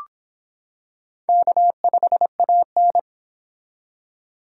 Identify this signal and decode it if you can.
Morse code, heard as a keyed tone, about 25 wpm, 700 Hz. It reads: K5AN